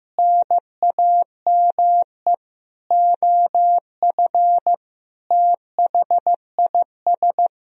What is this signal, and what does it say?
Morse code, heard as a keyed tone, about 15 wpm, 700 Hz. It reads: NAME OF THIS
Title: Sentences from Top 1000 Words - Review Part 1 - 15wpm